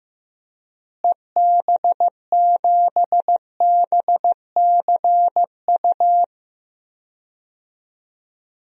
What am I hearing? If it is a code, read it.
Morse code, heard as a keyed tone, about 16 wpm, 700 Hz. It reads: EB7BCU
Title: DXCC-100 Call Signs - 6 Characters - Repeated 3x - 15wpm